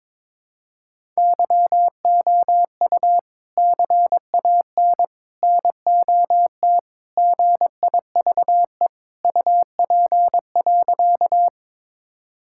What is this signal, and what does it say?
Morse code, heard as a keyed tone, about 23 wpm, 700 Hz. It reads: YOU CAN NOT GIVE UP.
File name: practice / 22wpm / Sentences with 3-Word Phrases - 22wpm